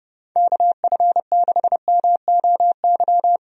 Morse code, heard as a keyed tone, about 31 wpm, 700 Hz. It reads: KF6MOY
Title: USA Call Signs - 6 Characters - 30wpm